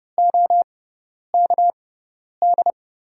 Morse code, heard as a keyed tone, about 32 wpm, spaced as for 10 wpm, 700 Hz. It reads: OKD
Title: USA Call Signs - 5 Characters - 6x Character Spacing - 30wpm